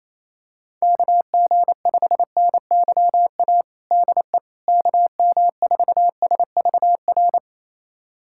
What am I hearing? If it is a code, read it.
Morse code, heard as a keyed tone, about 28 wpm, 700 Hz. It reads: KG5NYA DE KM4SVR
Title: USA Callsign DE Callsign - 6 Characters - 28wpm